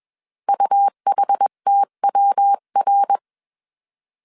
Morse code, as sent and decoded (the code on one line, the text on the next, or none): ..- .... - .-- .-.
UHTWR